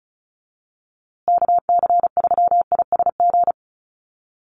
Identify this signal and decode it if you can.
Morse code, heard as a keyed tone, about 34 wpm, 700 Hz. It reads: KC3ISG
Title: USA Call Signs Repeated 3x 35wpm